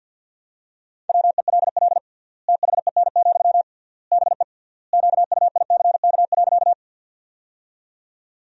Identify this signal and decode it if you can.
Morse code, heard as a keyed tone, about 47 wpm, 700 Hz. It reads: WELL THEN, BE QUICK.
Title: 3x Word-Spacing - Sentences from Top 500 Words - 50wpm